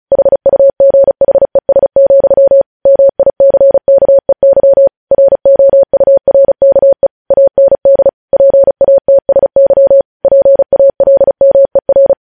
Morse code, heard as a keyed tone, about 36 wpm, 550 Hz. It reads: HUGHES, MICKEY ROURKE AND PATSY PALMER
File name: podcast / News headlines-1x-35x35